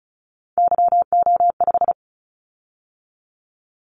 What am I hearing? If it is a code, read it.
Morse code, heard as a keyed tone, about 36 wpm, 700 Hz. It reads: YO5
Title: Callsign Prefixes for Top 25 DXCC - 35wpm